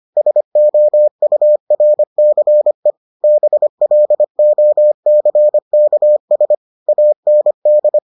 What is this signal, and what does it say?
Morse code, heard as a keyed tone, about 25 wpm, 600 Hz. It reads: SOURCE BLOCKS AND